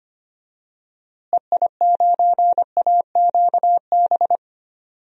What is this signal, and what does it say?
Morse code, heard as a keyed tone, about 25 wpm, 700 Hz. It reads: EI9AQB
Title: DXCC-100 Call Signs - 6 Characters - Repeated 2x - 25wpm